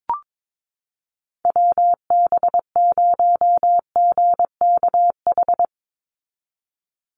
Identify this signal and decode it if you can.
Morse code, heard as a keyed tone, about 22 wpm, 700 Hz. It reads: WB0GKH
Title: Call Signs Repeated 3x - 22wpm